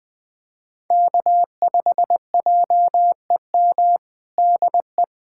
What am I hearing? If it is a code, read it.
Morse code, heard as a keyed tone, about 21 wpm, 700 Hz. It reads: K5JEM DE
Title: USA Callsign DE Callsign 20wpm